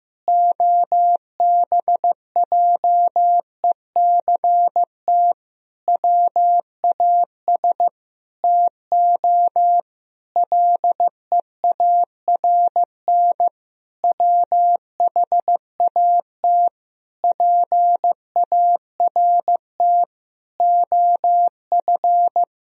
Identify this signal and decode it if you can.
Morse code, heard as a keyed tone, about 15 wpm, 700 Hz. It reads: OBJECT WAS TO LEARN WHAT PART OF
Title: Sentences from Top 1000 Words - Review Part 3 - 15wpm